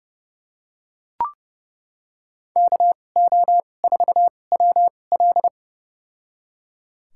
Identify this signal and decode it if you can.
Morse code, heard as a keyed tone, about 30 wpm, spaced as for 22 wpm, 700 Hz. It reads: KO4WL
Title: USA Call Signs - 5 Characters - 2x Character Spacing - 30wpm